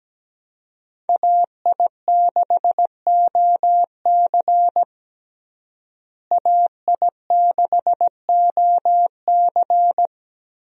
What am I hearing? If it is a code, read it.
Morse code, heard as a keyed tone, about 17 wpm, 700 Hz. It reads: AI6OC AI6OC
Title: USA Call Signs - 5 Characters - Repeated 2x - 17wpm